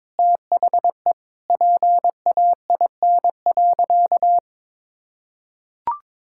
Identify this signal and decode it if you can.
Morse code, heard as a keyed tone, about 22 wpm, 700 Hz. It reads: THE PAIN.
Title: Sentences from Top 1600 Words 22wpm